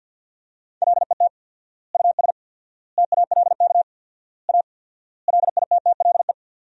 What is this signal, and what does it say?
Morse code, heard as a keyed tone, about 51 wpm, 700 Hz. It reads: LET US TALK A LITTLE
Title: 4x Word-Spacing - Sentences from Top 400 Words - 50wpm